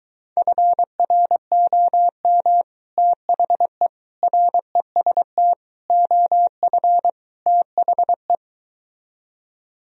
Morse code, heard as a keyed tone, about 23 wpm, 700 Hz. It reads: FROM THE REST OF THE